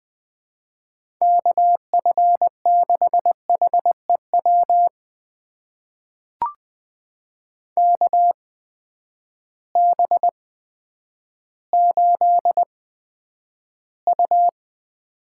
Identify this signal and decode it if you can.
Morse code, heard as a keyed tone, about 20 wpm, spaced as for 10 wpm, 700 Hz. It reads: KF6HEW KB8U
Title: USA Call Signs - 6 Characters - 8x Character Spacing - 20wpm